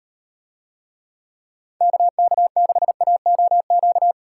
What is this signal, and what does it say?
Morse code, heard as a keyed tone, about 38 wpm, 700 Hz. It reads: KK6AOQ